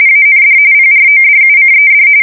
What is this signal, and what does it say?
RTTY, about 45.45 baud. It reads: AND PRODUCED